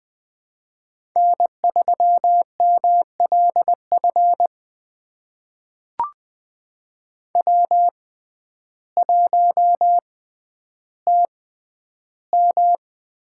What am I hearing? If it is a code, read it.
Morse code, heard as a keyed tone, about 20 wpm, spaced as for 11 wpm, 700 Hz. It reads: N3MLF W1TM
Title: USA Call Signs - 6x Character Spacing - 20wpm